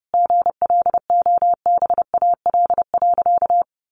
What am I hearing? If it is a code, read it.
Morse code, heard as a keyed tone, about 30 wpm, 700 Hz. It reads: GLOBAL.
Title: Sentences from Top 1200 Words 30wpm